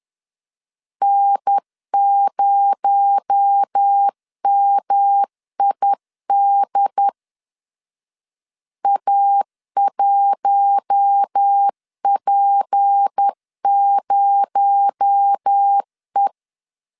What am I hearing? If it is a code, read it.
Morse code, heard as a keyed tone, about 10 wpm, 800 Hz. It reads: N0MID A1P0E
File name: LEZIONI / 6 PKX0 / 6 PKX0 10wpm 2707